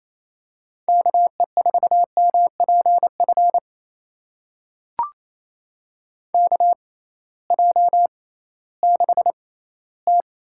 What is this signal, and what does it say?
Morse code, heard as a keyed tone, about 28 wpm, spaced as for 16 wpm, 700 Hz. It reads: KE4MPF KJ6T
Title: USA Call Signs - 6 Characters - 6x Character Spacing - 28wpm